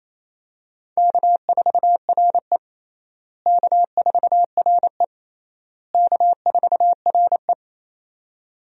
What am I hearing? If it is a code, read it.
Morse code, heard as a keyed tone, about 27 wpm, 700 Hz. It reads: K4RE K4RE K4RE
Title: USA Call Signs - 4 Characters - Repeated 3x - 28wpm